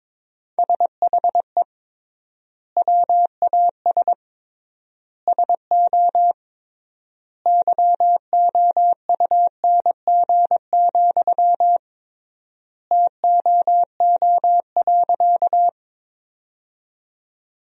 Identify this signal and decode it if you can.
Morse code, heard as a keyed tone, about 22 wpm, 700 Hz. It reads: SHE WAS SO YOUNG, TOO.
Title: 3x Word-Spacing - Sentences from Top 400 Words - 22wpm